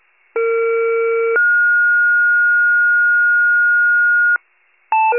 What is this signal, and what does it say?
Two-tone page, heard as a one-second tone, then a three-second tone, about 460 then 1430 Hz. It is heard from 0.4 to 4.4 s.